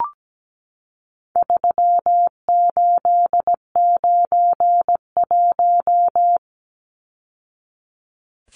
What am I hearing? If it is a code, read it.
Morse code, heard as a keyed tone, about 17 wpm, 700 Hz. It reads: 3891
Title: Numbers - 4-Digit 17wpm